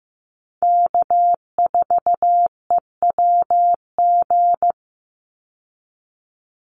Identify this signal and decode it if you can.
Morse code, heard as a keyed tone, about 15 wpm, 700 Hz. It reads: K4EWG
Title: Call Signs Repeated 3x - 15wpm